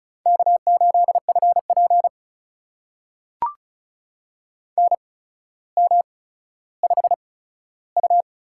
Morse code, heard as a keyed tone, about 36 wpm, spaced as for 15 wpm, 700 Hz. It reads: K8FP NM5U
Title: USA Call Signs - 4 Characters - 8x Character Spacing - 35wpm